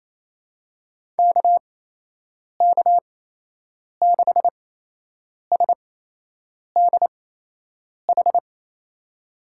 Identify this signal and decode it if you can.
Morse code, heard as a keyed tone, about 28 wpm, spaced as for 8 wpm, 700 Hz. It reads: KK6SDH
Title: USA Call Signs - 6 Characters - 8x Character Spacing - 28wpm